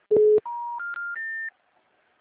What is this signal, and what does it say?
Telephone call progress tone, heard as a telephone network's tone. Special information tone.